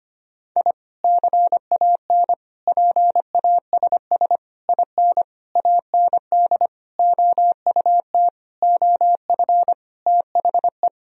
Morse code, heard as a keyed tone, about 25 wpm, 700 Hz. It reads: I CAN PASS IN AND OUT OF THE
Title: Sentences from Top 1000 Words - Review Part 4 - 25wpm